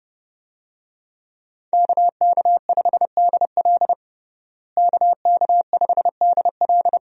Morse code, heard as a keyed tone, about 32 wpm, 700 Hz. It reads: KK5DL KK5DL
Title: USA Call Signs - 5 Characters - Repeated 3x - 30wpm